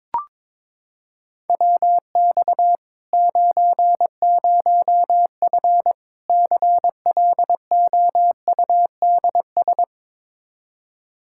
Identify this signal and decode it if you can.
Morse code, heard as a keyed tone, about 22 wpm, 700 Hz. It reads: WX 90F CLOUDS
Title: Practice QSO - Exchange 2 - Info - 22wpm